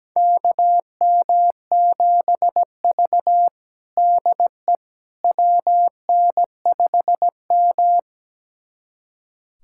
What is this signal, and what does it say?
Morse code, heard as a keyed tone, about 17 wpm, 700 Hz. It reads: KM7V DE WN5M
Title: USA Callsign DE Callsign - 4 Characters - 17wpm